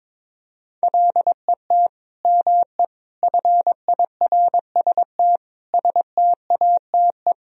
Morse code, heard as a keyed tone, about 22 wpm, 700 Hz. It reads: LET ME FIRST STATE
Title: Sentences from Top 1000 Words - Review Part 2 - 22wpm